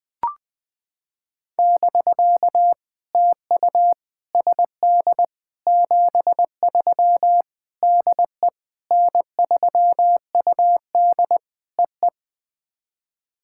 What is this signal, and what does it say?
Morse code, heard as a keyed tone, about 20 wpm, 700 Hz. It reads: <BK> TU SD 73 DE N3UD EE
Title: POTA QSO Practice - 20wpm